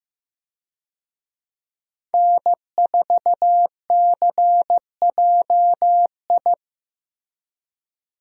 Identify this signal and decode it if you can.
Morse code, heard as a keyed tone, about 15 wpm, 700 Hz. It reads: N4CJI
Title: USA Call Signs - 5 Characters - Repeated 2x - 15wpm